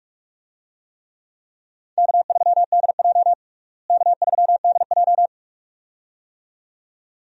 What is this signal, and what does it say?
Morse code, heard as a keyed tone, about 46 wpm, 700 Hz. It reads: K3DJ K3DJ
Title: USA Call Signs - 4 Characters - Repeated 2x - 45wpm